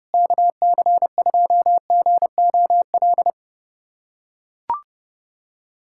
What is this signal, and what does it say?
Morse code, heard as a keyed tone, about 32 wpm, 700 Hz. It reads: KC2GOL
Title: USA Callsign DE Callsign 30wpm